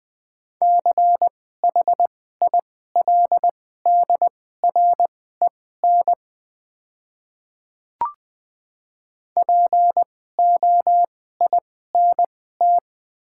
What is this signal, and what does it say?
Morse code, heard as a keyed tone, about 20 wpm, spaced as for 14 wpm, 690 Hz. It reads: CHILDREN POINT
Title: 2x Character Spacing - Top 500 Words - 20wpm